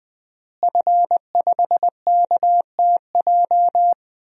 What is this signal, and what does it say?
Morse code, heard as a keyed tone, about 20 wpm, 700 Hz. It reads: F5KTJ
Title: DXCC-100 Call Signs - 5 Characters - Repeated 3x - 20wpm